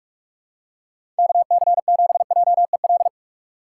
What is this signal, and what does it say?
Morse code, heard as a keyed tone, about 44 wpm, 700 Hz. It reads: KC7JEL
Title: USA Call Signs - 6 Characters - 6x Character Spacing - 45wpm